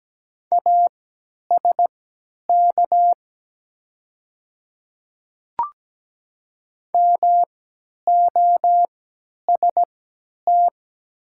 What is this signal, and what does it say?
Morse code, heard as a keyed tone, about 17 wpm, spaced as for 10 wpm, 700 Hz. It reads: ASK MOST
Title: 3x Character Spacing - Top 200 Words - 17wpm